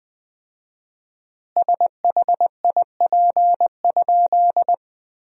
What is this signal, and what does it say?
Morse code, heard as a keyed tone, about 20 wpm, 700 Hz. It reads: SHIP?
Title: 4x Word-Spacing - Sentences from Top 400 Words - 20wpm